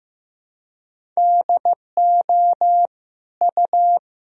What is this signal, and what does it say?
Morse code, heard as a keyed tone, about 15 wpm, 700 Hz. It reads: DO U
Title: Sentences from Top 400 Words and Abbreviations 15wpm